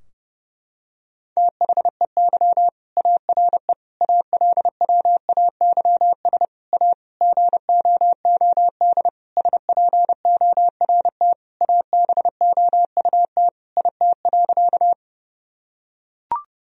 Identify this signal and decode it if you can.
Morse code, heard as a keyed tone, about 30 wpm, 700 Hz. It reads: THEY ARE ALWAYS A GOOD SPORT ABOUT IT.
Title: Sentences from Top 1000 Words - Review Part 4 - 30wpm